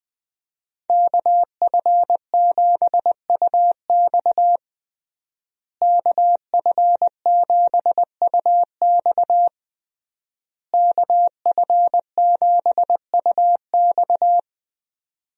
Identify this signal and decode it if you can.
Morse code, heard as a keyed tone, about 20 wpm, 700 Hz. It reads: KF7UX KF7UX KF7UX
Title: USA Call Signs - 5 Characters - Repeated 3x - 20wpm